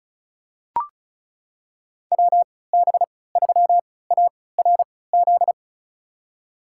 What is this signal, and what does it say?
Morse code, heard as a keyed tone, about 34 wpm, spaced as for 20 wpm, 700 Hz. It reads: WB3ARZ
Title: USA Call Signs - 6 Characters - 3x Character Spacing - 35wpm